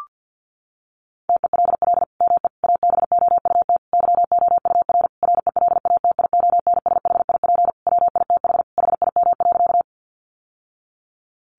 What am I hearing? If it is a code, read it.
Morse code, heard as a keyed tone, about 49 wpm, 700 Hz. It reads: TELL ME ABOUT YOUR RELATIONSHIP WITH HIM.
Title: Sentences from Top 1100 Words 50wpm